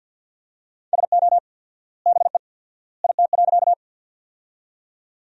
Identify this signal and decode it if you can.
Morse code, heard as a keyed tone, about 50 wpm, 700 Hz. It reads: SO BE IT.
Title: 4x Word-Spacing - Sentences from Top 100 Words - 50wpm